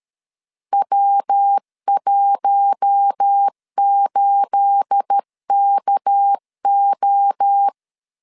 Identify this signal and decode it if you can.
Morse code, heard as a keyed tone, about 13 wpm, 800 Hz. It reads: W18KO
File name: CW / DET / 1 9 1500C S C 12wpm 3811